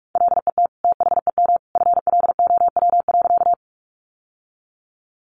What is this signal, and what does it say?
Morse code, heard as a keyed tone, about 44 wpm, 700 Hz. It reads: LET THEM FLOW.
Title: Sentences from Top 1200 Words 45wpm